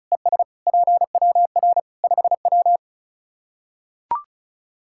Morse code, heard as a keyed tone, about 36 wpm, 700 Hz. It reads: ES PWR 5W
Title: Practice QSO - Exchange 2 - Info - 35wpm